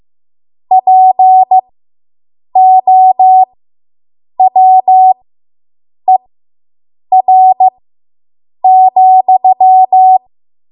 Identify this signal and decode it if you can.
Morse code, heard as a keyed tone, about 15 wpm, spaced as for 7 wpm, 750 Hz. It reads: POWER,